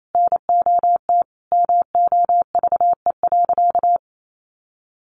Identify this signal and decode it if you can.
Morse code, heard as a keyed tone, about 28 wpm, 700 Hz. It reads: NOT MOVE.
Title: Sentences from Top 600 Words 28wpm